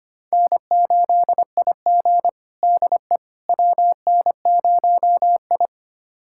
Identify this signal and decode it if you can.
Morse code, heard as a keyed tone, about 25 wpm, 700 Hz. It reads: N8IG DE WN0I
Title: USA Callsign DE Callsign - 4 Characters - 25wpm